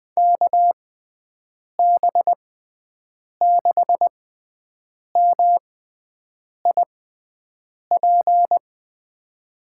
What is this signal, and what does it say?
Morse code, heard as a keyed tone, about 20 wpm, spaced as for 7 wpm, 700 Hz. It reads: KB6MIP
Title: USA Call Signs - 6x Character Spacing - 20wpm